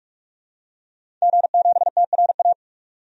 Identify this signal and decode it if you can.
Morse code, heard as a keyed tone, about 42 wpm, 700 Hz. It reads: G7TRA
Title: DXCC-100 Call Signs - 5 Characters - 45wpm